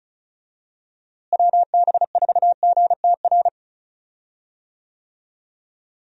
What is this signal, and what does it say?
Morse code, heard as a keyed tone, about 34 wpm, 700 Hz. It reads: WB4GTR
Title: USA Call Signs - 6 Characters - Repeated 3x - 35wpm